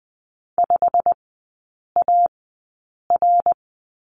Morse code, heard as a keyed tone, about 20 wpm, spaced as for 8 wpm, 700 Hz. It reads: 5AR
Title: 087 Lesson 15e - Number 5 - Three Characters 20wpm